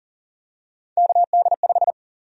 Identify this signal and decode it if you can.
Morse code, heard as a keyed tone, about 40 wpm, 700 Hz. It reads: KD5
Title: USA Call Sign Prefixes - 40wpm